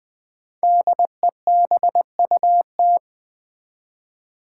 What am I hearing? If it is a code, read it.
Morse code, heard as a keyed tone, about 20 wpm, 700 Hz. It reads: DEBUT
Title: Build A Word - 1 Letter At A Time - 5-Letter Words - 20wpm